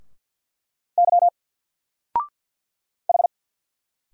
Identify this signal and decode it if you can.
Morse code, heard as a keyed tone, about 50 wpm, 700 Hz. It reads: Y H
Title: Single Letter-Number 50wpm